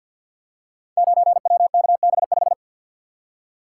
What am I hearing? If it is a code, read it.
Morse code, heard as a keyed tone, about 49 wpm, 700 Hz. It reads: 9WKB5
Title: Code Groups - 5 Letter:Numbers - 50wpm